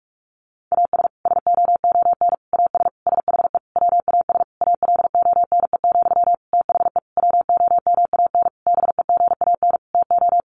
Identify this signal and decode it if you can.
Morse code, heard as a keyed tone, about 44 wpm, 700 Hz. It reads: AS SOON AS SHE WAS ALONE, THE WOMAN BEGAN TO
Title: Sentences from Top 1100 Words 45wpm